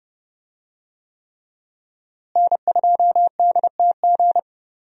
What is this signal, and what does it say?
Morse code, heard as a keyed tone, about 30 wpm, 700 Hz. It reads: N2DTG
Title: USA Call Signs - 5 Characters - Repeated 3x - 30wpm